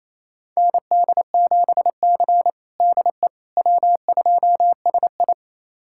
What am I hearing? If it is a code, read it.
Morse code, heard as a keyed tone, about 28 wpm, 700 Hz. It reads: ND7C DE W2SI
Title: USA Callsign DE Callsign - 4 Characters - 28wpm